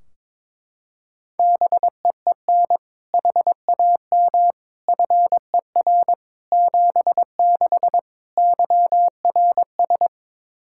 Morse code, heard as a keyed tone, about 21 wpm, 700 Hz. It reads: BEEN HAM FER 76 YRS
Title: Practice QSO - Exchange 3 - More Info - 22wpm